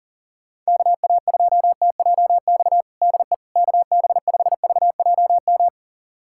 Morse code, heard as a keyed tone, about 40 wpm, 700 Hz. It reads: KA2TJX DE KB5VJM
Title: USA Callsign DE Callsign 40wpm